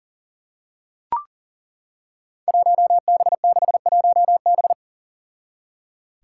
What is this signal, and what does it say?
Morse code, heard as a keyed tone, about 37 wpm, 700 Hz. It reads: 1B61B
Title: Five Challenging Letter-Number Combinations 40wpm